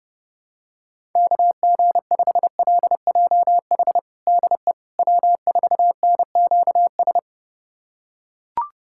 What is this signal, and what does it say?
Morse code, heard as a keyed tone, about 30 wpm, 700 Hz. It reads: KG5LJH DE W4NQS
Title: USA Callsign DE Callsign 30wpm